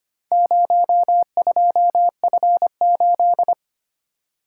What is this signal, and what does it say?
Morse code, heard as a keyed tone, about 25 wpm, 690 Hz. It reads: 02F8
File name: practice / 25wpm / Four Letter-Number Combinations - 25wpm